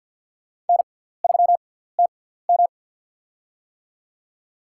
Morse code, heard as a keyed tone, about 52 wpm, spaced as for 17 wpm, 700 Hz. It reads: N3TM
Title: USA Call Signs - 4 Characters - 6x Character Spacing - 50wpm